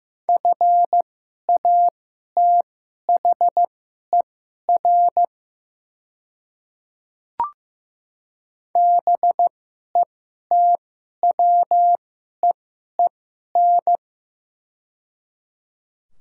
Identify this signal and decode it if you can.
Morse code, heard as a keyed tone, about 16 wpm, spaced as for 11 wpm, 700 Hz. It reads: FATHER BETWEEN
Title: 2x Character Spacing - Top 500 Words - 15wpm